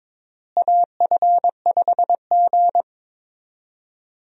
Morse code, heard as a keyed tone, about 22 wpm, 700 Hz. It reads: AF5G
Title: USA Call Signs - 4 Characters - Repeated 2x - 22wpm